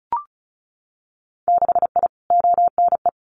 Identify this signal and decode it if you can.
Morse code, heard as a keyed tone, about 36 wpm, 700 Hz. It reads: 6I ONE
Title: Winter Field Day Exchanges Encore 35wpm